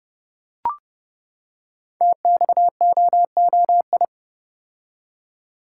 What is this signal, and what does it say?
Morse code, heard as a keyed tone, about 30 wpm, 700 Hz. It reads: TXOOI
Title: Code Groups - 5 Letters - 30wpm